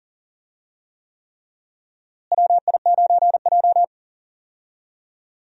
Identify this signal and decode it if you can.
Morse code, heard as a keyed tone, about 37 wpm, 700 Hz. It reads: WI9J
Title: USA Call Signs - 4 Characters - Repeated 3x - 40wpm